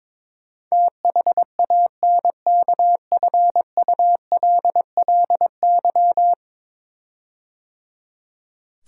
Morse code, heard as a keyed tone, about 22 wpm, 700 Hz. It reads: THANKFULLY
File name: practice / 22wpm / 10 Letter Words - First 3 Letters Given - 22wpm